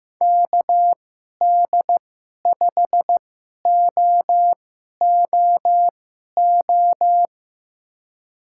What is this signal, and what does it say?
Morse code, heard as a keyed tone, about 15 wpm, spaced as for 11 wpm, 700 Hz. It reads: KD5OOO
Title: USA Call Signs - 6 Characters - 2x Character Spacing - 15wpm